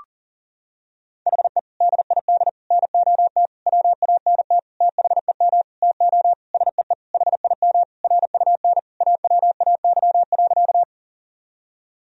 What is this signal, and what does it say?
Morse code, heard as a keyed tone, about 40 wpm, 700 Hz. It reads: HE DID NOT WANT THEM TO SEE HIM RUN AWAY.
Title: Sentences from Top 1000 Words - Review Part 1 - 40wpm